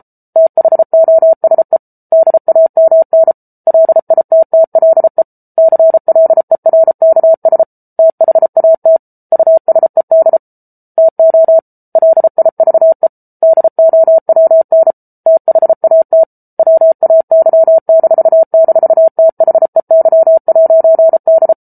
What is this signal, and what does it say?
Morse code, heard as a keyed tone, about 34 wpm, 650 Hz. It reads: THOSE DAMN LITTLE CLERKS THAT USED TO LIVE DOWN THAT WAY--THEY'D